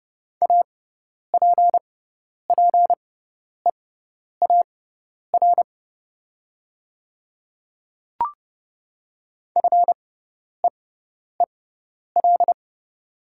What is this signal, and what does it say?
Morse code, heard as a keyed tone, about 30 wpm, spaced as for 10 wpm, 700 Hz. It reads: APPEAR FEEL
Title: 6x Character Spacing - Top 500 Words - 30wpm